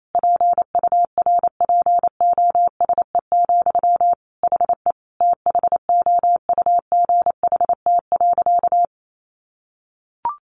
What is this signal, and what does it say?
Morse code, heard as a keyed tone, about 28 wpm, 700 Hz. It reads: PURPOSE, HE THOUGHT.
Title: Sentences from Top 800 Words 28wpm